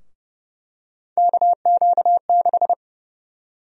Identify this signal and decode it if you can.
Morse code, heard as a keyed tone, about 30 wpm, 700 Hz. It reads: KQ6
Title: USA Call Sign Prefixes - 30wpm